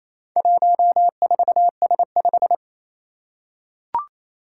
Morse code, heard as a keyed tone, about 27 wpm, 700 Hz. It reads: 14S5
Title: Four Challenging Letter-Number Combinations 28wpm